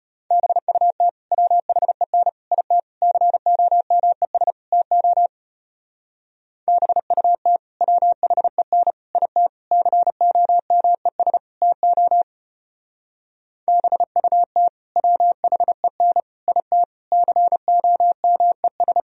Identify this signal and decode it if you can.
Morse code, heard as a keyed tone, about 32 wpm, 700 Hz. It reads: BUT WHEN IT COMES TO BUT WHEN IT COMES TO BUT WHEN IT COMES